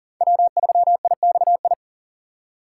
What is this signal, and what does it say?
Morse code, heard as a keyed tone, about 40 wpm, 700 Hz. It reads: W3IXI
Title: USA Call Signs - 5 Characters - 4x Character Spacing - 40wpm